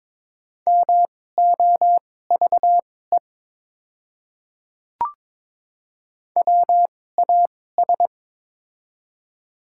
Morse code, heard as a keyed tone, about 22 wpm, spaced as for 16 wpm, 700 Hz. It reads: MOVE WAS
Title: 2x Character Spacing - Top 200 Words - 22wpm